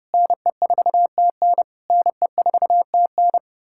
Morse code, heard as a keyed tone, about 33 wpm, 700 Hz. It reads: NE4TN NE4TN